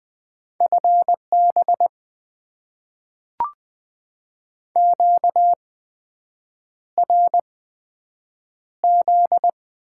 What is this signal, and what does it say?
Morse code, heard as a keyed tone, about 20 wpm, spaced as for 7 wpm, 700 Hz. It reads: FB QRZ
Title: Top 90 QSO Elements - 8x Character Spacing - 20wpm